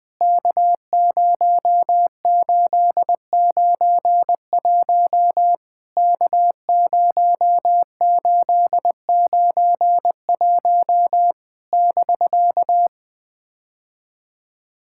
Morse code, heard as a keyed tone, about 20 wpm, 700 Hz. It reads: K0891 K0891 <BK>
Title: POTA QSO Practice - 20wpm